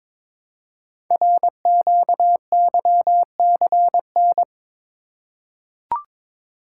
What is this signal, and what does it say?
Morse code, heard as a keyed tone, about 22 wpm, 700 Hz. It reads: RQYCN
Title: Code Groups - 5 Letters - 22wpm